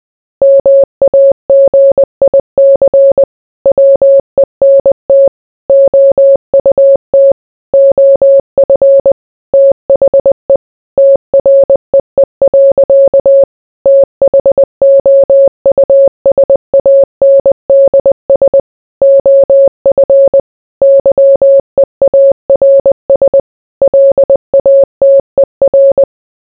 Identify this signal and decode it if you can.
Morse code, heard as a keyed tone, about 20 wpm, 550 Hz. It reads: MAGIC WENT OUT OF THE TREE. THOUSANDS OF YEARS LATER